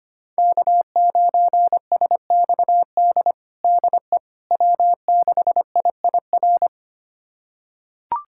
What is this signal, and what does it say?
Morse code, heard as a keyed tone, about 25 wpm, 700 Hz. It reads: K9SXD DE W6IIR
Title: USA Callsign DE Callsign - 5 Characters - 25wpm